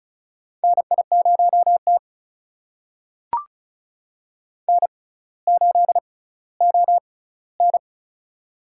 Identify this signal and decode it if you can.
Morse code, heard as a keyed tone, about 36 wpm, spaced as for 18 wpm, 700 Hz. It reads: NI0T N8ON
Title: USA Call Signs - 4 Characters - 6x Character Spacing - 35wpm